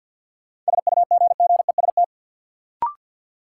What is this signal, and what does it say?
Morse code, heard as a keyed tone, about 53 wpm, 700 Hz. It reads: SUGGEST
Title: Sentences from Top 1300 Words - New Word Drill Alt - 50wpm